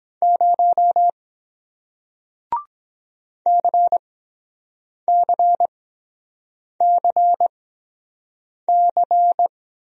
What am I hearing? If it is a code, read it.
Morse code, heard as a keyed tone, about 23 wpm, spaced as for 7 wpm, 700 Hz. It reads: 0 CCCC